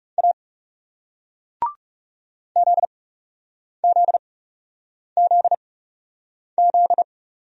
Morse code, heard as a keyed tone, about 37 wpm, spaced as for 8 wpm, 700 Hz. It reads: A ZZZZ
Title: Speed-Racing Single Letter-Number 30wpm